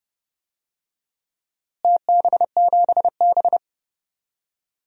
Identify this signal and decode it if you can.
Morse code, heard as a keyed tone, about 30 wpm, 700 Hz. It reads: TB7B
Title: DXCC-100 Call Signs - 4 Characters - Repeated 2x - 30wpm